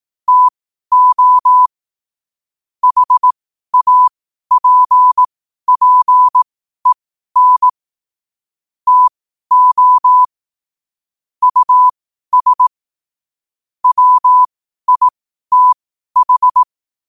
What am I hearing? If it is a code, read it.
Morse code, heard as a keyed tone, about 18 wpm, spaced as for 12 wpm, 1000 Hz. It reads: TO HAPPEN TO US WITH
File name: qotdfiles / 12wpm